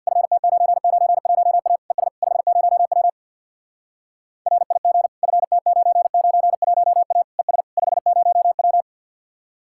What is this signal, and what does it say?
Morse code, heard as a keyed tone, about 51 wpm, 700 Hz. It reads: FT991A ES 50W RIG FT991A ES 50W
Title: Speed-Racing Common QSO Phrases - 50wpm